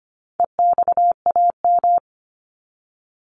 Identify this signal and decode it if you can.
Morse code, heard as a keyed tone, about 25 wpm, 700 Hz. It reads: EXAM
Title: Top 501-750 Words 25wpm